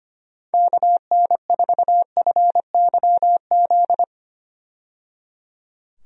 Morse code, heard as a keyed tone, about 25 wpm, 700 Hz. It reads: KN4FYZ
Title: USA Call Signs - 6 Characters - 25wpm